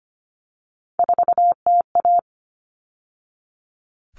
Morse code, heard as a keyed tone, about 25 wpm, 700 Hz. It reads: 4TA
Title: All Cut Numbers - 3-Digit 25wpm